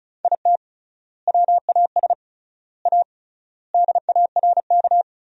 Morse code, heard as a keyed tone, about 34 wpm, 700 Hz. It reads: IT WAS A DARK
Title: 3x Word-Spacing - Sentences from Top 500 Words - 35wpm